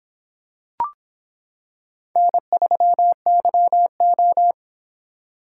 Morse code, heard as a keyed tone, about 25 wpm, 700 Hz. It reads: N3YO